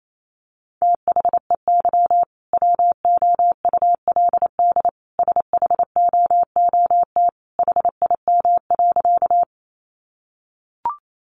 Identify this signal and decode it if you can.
Morse code, heard as a keyed tone, about 28 wpm, 700 Hz. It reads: THEY WOULD SHOOT HIM.
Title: Sentences from Top 1200 Words 28wpm